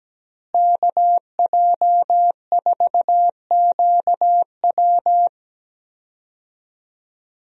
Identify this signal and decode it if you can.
Morse code, heard as a keyed tone, about 17 wpm, 700 Hz. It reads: KJ4QW
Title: USA Call Signs - 5 Characters - Repeated 3x - 17wpm